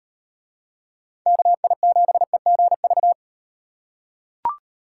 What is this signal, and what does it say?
Morse code, heard as a keyed tone, about 38 wpm, 700 Hz. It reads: KI7EGV